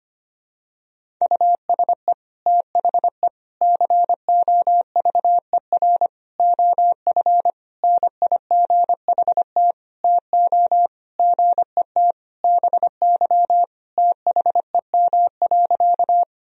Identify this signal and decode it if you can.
Morse code, heard as a keyed tone, about 25 wpm, 700 Hz. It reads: USE THE COVER OF NIGHT TO GET BY THEM.
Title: Sentences from Top 1000 Words - Review Part 1 - 25wpm